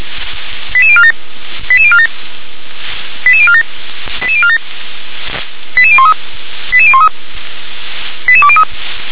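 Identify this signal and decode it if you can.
Five-tone selective call (ZVEI), heard as quick runs of five tones, tones of about 70 ms. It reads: 88047 88047 88047 88047 88013 88013 88203
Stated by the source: ZVEI five-tone selective call received on an SDR.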